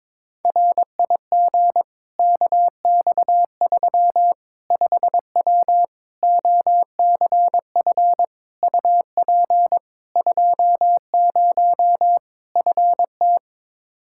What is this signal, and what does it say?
Morse code, heard as a keyed tone, about 21 wpm, 700 Hz. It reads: RIG KX3 5W OCF UP 20 FT
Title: Practice QSO - Exchange 2 - Info - 22wpm